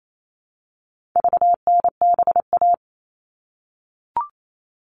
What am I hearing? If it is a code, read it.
Morse code, heard as a keyed tone, about 28 wpm, 700 Hz. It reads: VNBA
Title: All Cut Numbers - 4-Digit 28wpm